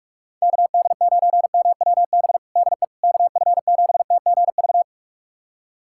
Morse code, heard as a keyed tone, about 46 wpm, 700 Hz. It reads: KD9MWB DE KF7TGV
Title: USA Callsign DE Callsign - 6 Characters - 45wpm